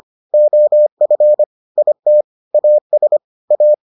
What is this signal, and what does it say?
Morse code, heard as a keyed tone, about 25 wpm, 600 Hz. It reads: OF IT AS A